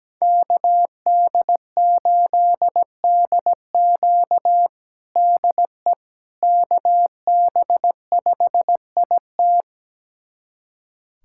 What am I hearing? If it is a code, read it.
Morse code, heard as a keyed tone, about 17 wpm, 700 Hz. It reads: KD8DQ DE KB5IT
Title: USA Callsign DE Callsign - 5 Characters - 17wpm